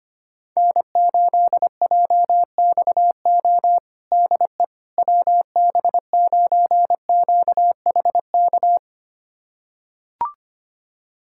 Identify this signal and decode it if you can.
Morse code, heard as a keyed tone, about 25 wpm, 700 Hz. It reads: N8JXO DE WB9QHK
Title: USA Callsign DE Callsign 25wpm